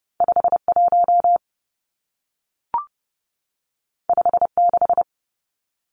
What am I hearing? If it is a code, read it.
Morse code, heard as a keyed tone, about 30 wpm, 700 Hz. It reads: 5 1 5 6
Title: Two Challenging Letter-Number Combinations 30wpm